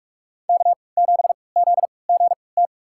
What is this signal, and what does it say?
Morse code, heard as a keyed tone, about 47 wpm, spaced as for 26 wpm, 700 Hz. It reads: K7ZGT
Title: USA Call Signs - 5 Characters - 3x Character Spacing - 45wpm